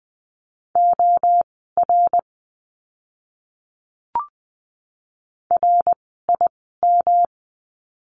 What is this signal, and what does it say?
Morse code, heard as a keyed tone, about 20 wpm, spaced as for 15 wpm, 700 Hz. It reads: OR RIM
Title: 108 Lesson 18f - Letter m - Words 20wpm